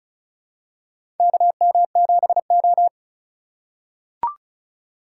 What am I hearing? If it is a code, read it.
Morse code, heard as a keyed tone, about 36 wpm, 700 Hz. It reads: KM7O
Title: USA Call Signs - 4 Characters - 4x Character Spacing - 35wpm